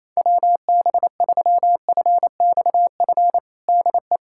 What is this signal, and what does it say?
Morse code, heard as a keyed tone, about 28 wpm, 700 Hz. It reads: WB3FXF DE
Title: USA Callsign DE Callsign 28wpm